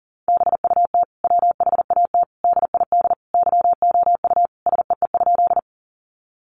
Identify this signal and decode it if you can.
Morse code, heard as a keyed tone, about 40 wpm, 700 Hz. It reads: BUT WHAT DID YOU SEE?
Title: Sentences from Top 100 Words 40wpm